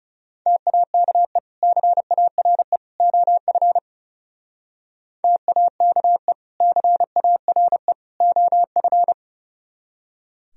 Morse code, heard as a keyed tone, about 36 wpm, 700 Hz. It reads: TAKE CARE OF TAKE CARE OF
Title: Speed-Racing - Sets of 3 Words 30wpm